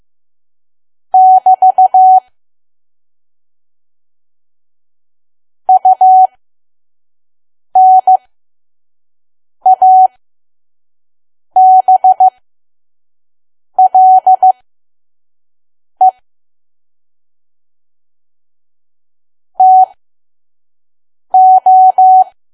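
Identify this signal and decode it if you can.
Morse code, heard as a keyed tone, about 15 wpm, spaced as for 5 wpm, 750 Hz. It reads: <BT> UNABLE TO
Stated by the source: Morse code practice at 5 WPM from KQ2P.COM.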